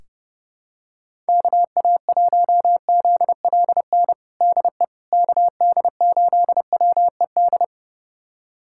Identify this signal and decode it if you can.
Morse code, heard as a keyed tone, about 30 wpm, 700 Hz. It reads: KA1ZLN DE KD8WED
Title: USA Callsign DE Callsign - 6 Characters - 30wpm